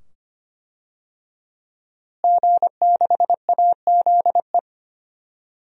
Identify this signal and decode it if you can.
Morse code, heard as a keyed tone, about 25 wpm, 700 Hz. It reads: G6AZE
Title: DXCC-100 Call Signs - 5 Characters - Repeated 2x - 25wpm